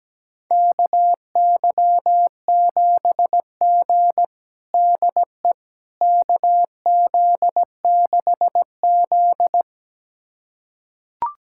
Morse code, heard as a keyed tone, about 17 wpm, 700 Hz. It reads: KY7G DE KZ6Z
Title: USA Callsign DE Callsign - 4 Characters - 17wpm